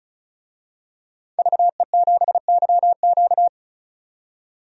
Morse code, heard as a keyed tone, about 36 wpm, 700 Hz. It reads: VE7YQ
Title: DXCC-100 Call Signs - 5 Characters - 35wpm